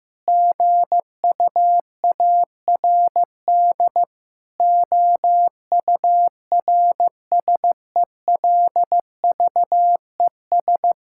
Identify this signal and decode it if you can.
Morse code, heard as a keyed tone, about 16 wpm, 700 Hz. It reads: GUARD OURSELVES
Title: Sentences from Top 1600 Words 15wpm